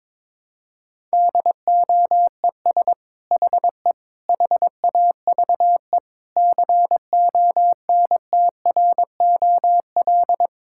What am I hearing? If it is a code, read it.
Morse code, heard as a keyed tone, about 22 wpm, 700 Hz. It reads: DOES HE HAVE CONTROL
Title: Sentences from Top 1000 Words - Review Part 5 - 22wpm